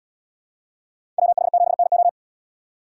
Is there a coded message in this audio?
Yes: Morse code from 1.2 to 2.1 s, heard as a keyed tone, about 72 wpm, 700 Hz. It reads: WH7NO